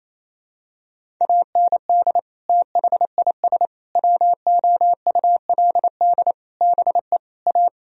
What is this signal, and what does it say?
Morse code, heard as a keyed tone, about 28 wpm, 700 Hz. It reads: AND THIS WOULD BE A
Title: Sentences from Top 1000 Words - Review Part 1 - 28wpm